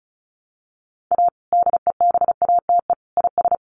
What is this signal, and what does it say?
Morse code, heard as a keyed tone, about 34 wpm, 700 Hz. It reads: A DEBATE IS
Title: Sentences from Top 1100 Words 35wpm